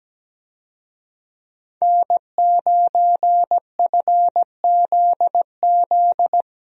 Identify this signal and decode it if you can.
Morse code, heard as a keyed tone, about 17 wpm, 700 Hz. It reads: N9FZZ
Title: USA Call Signs - 5 Characters - Repeated 3x - 17wpm